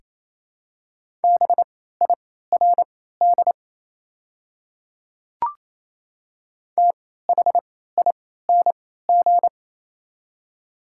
Morse code, heard as a keyed tone, about 27 wpm, spaced as for 16 wpm, 700 Hz. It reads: BIRD THING
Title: 3x Character Spacing - Top 500 Words - 28wpm